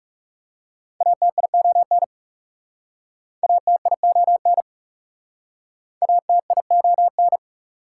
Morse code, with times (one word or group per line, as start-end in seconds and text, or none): 1.0-2.0 ATION
3.4-4.6 ATION
6.0-7.4 ATION